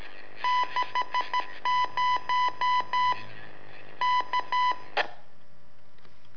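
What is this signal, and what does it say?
Morse code, heard as a keyed tone, about 13 wpm, 1000 Hz. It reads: 60 K
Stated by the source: Telemetry in Morse from LU7AA and LU4EZ, AMSAT and Turma Mar del Plata, sent from a balloon.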